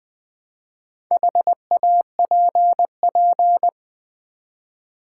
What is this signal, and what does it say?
Morse code, heard as a keyed tone, about 20 wpm, 700 Hz. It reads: HAPP
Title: Build A Word - 1 Letter At A Time - 5-Letter Words - 20wpm